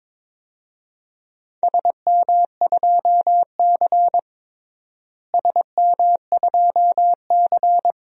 Morse code, heard as a keyed tone, about 22 wpm, 700 Hz. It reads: SM2C SM2C
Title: DXCC-100 Call Signs - 4 Characters - Repeated 2x - 22wpm